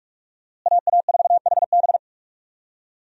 Morse code, heard as a keyed tone, about 48 wpm, 700 Hz. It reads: AA4HB
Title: USA Call Signs - 5 Characters - Repeated 2x - 45wpm